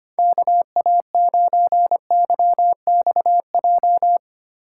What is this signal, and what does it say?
Morse code, heard as a keyed tone, about 30 wpm, 700 Hz. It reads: KA9YXJ
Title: Speed Racing - USA 2x3 Call Signs 25wpm